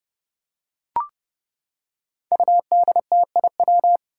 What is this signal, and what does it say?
Morse code, heard as a keyed tone, about 30 wpm, 700 Hz. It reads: UDTIW
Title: Five Challenging Letter-Number Combinations 30wpm